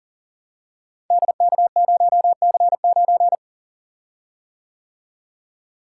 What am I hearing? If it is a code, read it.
Morse code, heard as a keyed tone, about 41 wpm, 700 Hz. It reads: DK0C9
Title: Code Groups - 5 Letter:Numbers - 40wpm